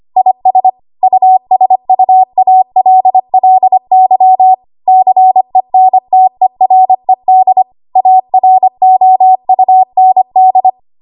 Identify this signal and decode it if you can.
Morse code, heard as a keyed tone, about 25 wpm, 750 Hz. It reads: IS USUALLY CENTERED AROUND
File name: Archive / 25 WPM / 25WPM